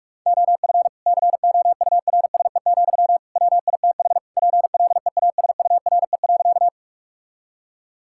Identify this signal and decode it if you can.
Morse code, heard as a keyed tone, about 41 wpm, 700 Hz. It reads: OF COURSE, WITH PLEASURE.
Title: Sentences from Top 1600 Words 45wpm